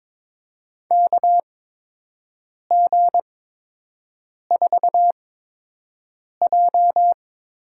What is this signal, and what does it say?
Morse code, heard as a keyed tone, about 22 wpm, spaced as for 6 wpm, 700 Hz. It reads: KG4J
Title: USA Call Signs - 8x Character Spacing - 22wpm